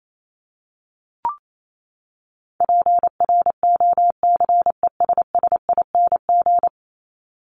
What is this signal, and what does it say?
Morse code, heard as a keyed tone, about 28 wpm, 700 Hz. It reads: PROCESSING
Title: ING Suffix - Top 100 Words with Base Words 28wpm